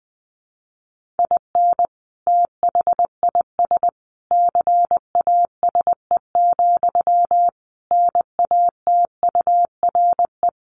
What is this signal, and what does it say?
Morse code, heard as a keyed tone, about 20 wpm, 700 Hz. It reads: IN THIS CASE, NATURE
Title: Sentences from Top 1400 Words 20wpm